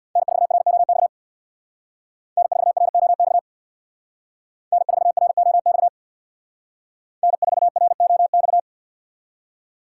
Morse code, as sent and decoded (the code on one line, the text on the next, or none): -. ....- .-. --- -..- / -. ....- .-. --- -..- / -. ....- .-. --- -..- / -. ....- .-. --- -..-
N4ROX N4ROX N4ROX N4ROX